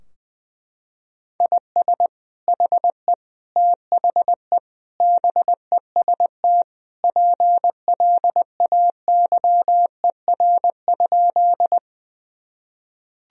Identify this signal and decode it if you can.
Morse code, heard as a keyed tone, about 21 wpm, 700 Hz. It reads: IS HE THE BEST PLAYER?
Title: Sentences from Top 1000 Words - Review Part 4 - 20wpm